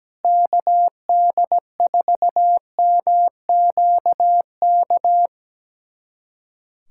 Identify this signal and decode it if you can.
Morse code, heard as a keyed tone, about 17 wpm, 700 Hz. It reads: KD4MQK
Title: USA Callsign DE Callsign - 6 Characters - 17wpm